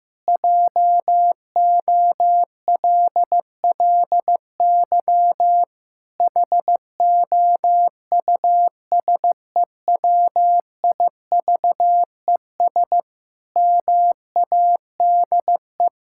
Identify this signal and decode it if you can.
Morse code, heard as a keyed tone, about 15 wpm, 700 Hz. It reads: JOLLY HOUSEWIVES MADE
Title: Pangram Challenge v2 - 15wpm